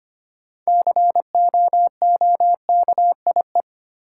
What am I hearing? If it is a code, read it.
Morse code, heard as a keyed tone, about 25 wpm, 700 Hz. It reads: COOKIE
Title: Six Letter Words - 25wpm